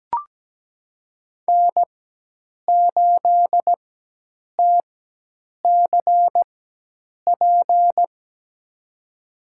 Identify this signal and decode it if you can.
Morse code, heard as a keyed tone, about 17 wpm, spaced as for 8 wpm, 700 Hz. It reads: N8TCP
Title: USA Call Signs - 5 Characters - 4x Character Spacing - 17wpm